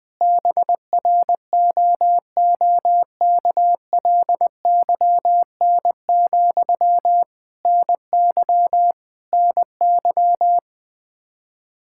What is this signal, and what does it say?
Morse code, heard as a keyed tone, about 20 wpm, 700 Hz. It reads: BROOKLYN, NY NY
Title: Practice QSO - Exchange 1 - Beginning - 20wpm